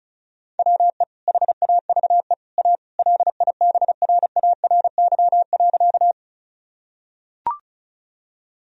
Morse code, heard as a keyed tone, about 36 wpm, 700 Hz. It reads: WE HAVE A LIBRARY.
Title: Sentences from Top 1000 Words - Review Part 4 - 35wpm